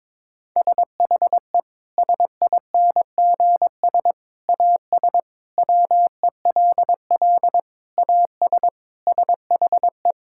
Morse code, heard as a keyed tone, about 22 wpm, 700 Hz. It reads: SHE SINGS AS WELL AS SHE
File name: practice / 22wpm / Sentences with 3-Word Phrases - 22wpm